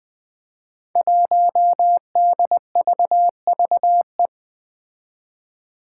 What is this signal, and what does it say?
Morse code, heard as a keyed tone, about 21 wpm, 700 Hz. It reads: 1DVVE
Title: Five Challenging Letter-Number Combinations 20wpm